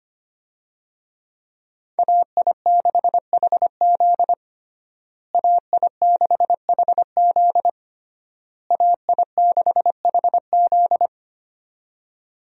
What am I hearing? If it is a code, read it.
Morse code, heard as a keyed tone, about 25 wpm, 700 Hz. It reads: AI6HZ AI6HZ AI6HZ
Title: USA Call Signs - 5 Characters - Repeated 3x - 25wpm